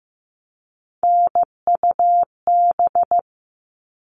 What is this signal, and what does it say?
Morse code, heard as a keyed tone, about 15 wpm, 700 Hz. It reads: NUB
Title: All Cut Numbers - 3-Digit 15wpm